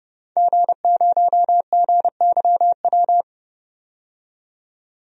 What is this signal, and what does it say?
Morse code, heard as a keyed tone, about 30 wpm, 700 Hz. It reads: G0GYW
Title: DXCC-100 Call Signs - 5 Characters - 30wpm